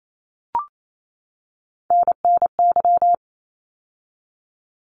Morse code, heard as a keyed tone, about 28 wpm, 700 Hz. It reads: NNY
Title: ARRL Sections 28wpm